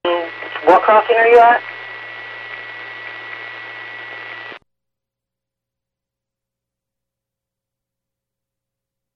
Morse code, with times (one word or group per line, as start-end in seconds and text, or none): none